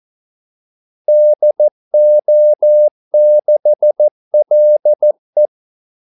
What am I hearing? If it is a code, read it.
Morse code, heard as a keyed tone, about 14 wpm, 600 Hz. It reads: DO6LE
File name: Callsigns-14wpm